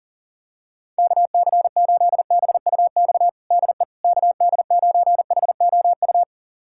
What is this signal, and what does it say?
Morse code, heard as a keyed tone, about 40 wpm, 700 Hz. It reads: KC8BUX DE KD9HOU
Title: USA Callsign DE Callsign - 6 Characters - 40wpm